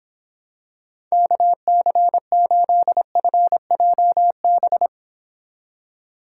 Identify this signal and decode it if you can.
Morse code, heard as a keyed tone, about 26 wpm, 700 Hz. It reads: KC8FJB